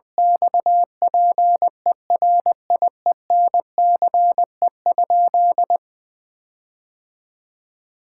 Morse code, heard as a keyed tone, about 21 wpm, 700 Hz. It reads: XPERIENCE?
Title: Sentences from Top 1600 Words 20wpm